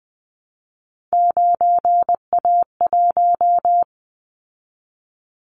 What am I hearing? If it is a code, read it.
Morse code, heard as a keyed tone, about 20 wpm, 700 Hz. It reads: 9A1
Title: Callsign Prefixes for Top 50 DXCC - 20wpm